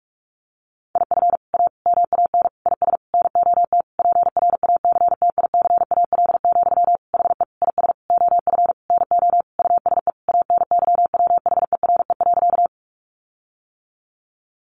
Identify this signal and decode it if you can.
Morse code, heard as a keyed tone, about 44 wpm, 700 Hz. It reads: IF A MAN IS NOT PRACTICAL, HE IS OF NO USE ANYWHERE.
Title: Sentences from Top 1400 Words 45wpm